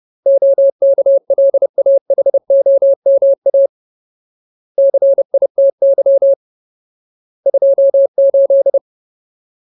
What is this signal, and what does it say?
Morse code, heard as a keyed tone, about 30 wpm, 550 Hz. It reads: OKLAHOMA CITY 28
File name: Ultimate Word List / 30WPM Ultimate